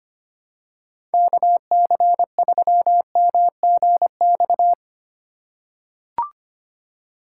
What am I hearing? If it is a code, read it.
Morse code, heard as a keyed tone, about 25 wpm, 700 Hz. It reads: KC3MGX
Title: USA Call Signs - 6 Characters - 4x Character Spacing - 25wpm